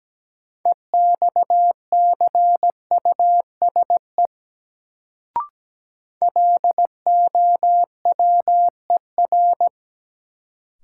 Morse code, heard as a keyed tone, about 17 wpm, 700 Hz. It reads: EXCUSE LOWER
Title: Sentences from Top 700 Words - New Word Drill - 17wpm